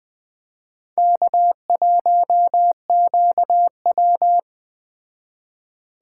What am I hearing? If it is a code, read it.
Morse code, heard as a keyed tone, about 20 wpm, 700 Hz. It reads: K1QW
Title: Speed Racing - USA 1x2 Call Signs 17wpm